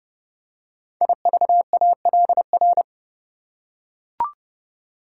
Morse code, heard as a keyed tone, about 30 wpm, 700 Hz. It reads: IVALR
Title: Code Groups - 5 Letters - 30wpm